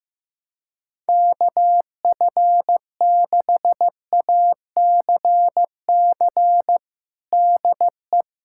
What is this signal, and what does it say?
Morse code, heard as a keyed tone, about 15 wpm, 700 Hz. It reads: KF6ACC DE
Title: USA Callsign DE Callsign 15wpm